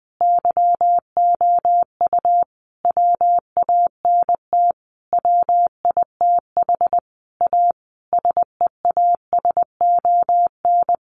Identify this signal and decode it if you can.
Morse code, heard as a keyed tone, about 20 wpm, 700 Hz. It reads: YOU WANT WITH A SEASON
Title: Sentences from Top 1400 Words 20wpm